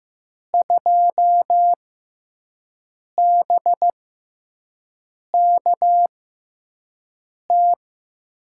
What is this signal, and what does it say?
Morse code, heard as a keyed tone, about 15 wpm, spaced as for 5 wpm, 700 Hz. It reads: 2BKT
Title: USA Call Signs - 6x Character Spacing - 15wpm